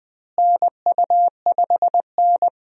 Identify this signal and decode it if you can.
Morse code, heard as a keyed tone, about 20 wpm, 700 Hz. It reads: NU5N
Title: USA Call Signs - 4 Characters - Repeated 3x - 20wpm